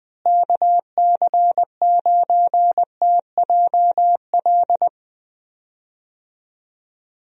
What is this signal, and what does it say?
Morse code, heard as a keyed tone, about 20 wpm, 700 Hz. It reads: KC9TJL
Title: USA Call Signs - 6 Characters - Repeated 3x - 20wpm